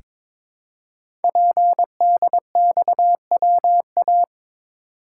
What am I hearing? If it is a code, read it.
Morse code, heard as a keyed tone, about 22 wpm, 700 Hz. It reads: PDXWA
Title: Code Groups - 5 Letter:Numbers - 22wpm